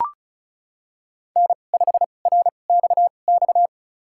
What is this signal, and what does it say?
Morse code, heard as a keyed tone, about 34 wpm, spaced as for 25 wpm, 700 Hz. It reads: N5RXX
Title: USA Call Signs - 5 Characters - 2x Character Spacing - 35wpm